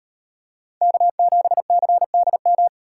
Morse code, heard as a keyed tone, about 34 wpm, 700 Hz. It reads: K7CDM